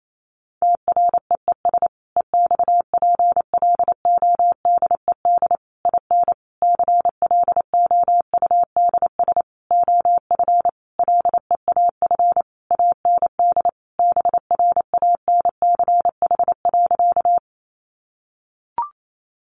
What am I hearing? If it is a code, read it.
Morse code, heard as a keyed tone, about 28 wpm, 700 Hz. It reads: TREES EXPLODED IN CLOUDS OF LEAF AND BRANCH.
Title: A Sound Of Thunder 28wpm